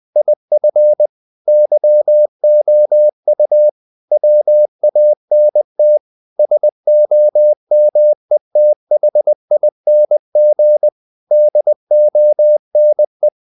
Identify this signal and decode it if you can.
Morse code, heard as a keyed tone, about 20 wpm, 600 Hz. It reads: IF YOU WANT SOMETHING DONE
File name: CW / Practice / EnglishSayings20